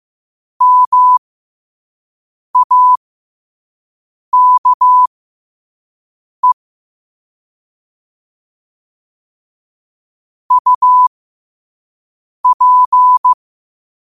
Morse code, heard as a keyed tone, about 15 wpm, spaced as for 5 wpm, 1000 Hz. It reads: MAKE UP
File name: 5wpm